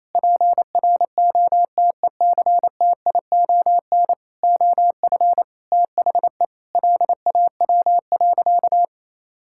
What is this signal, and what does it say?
Morse code, heard as a keyed tone, about 28 wpm, 700 Hz. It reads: PROTECTION OF THE LAW.
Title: Sentences from Top 1600 Words 28wpm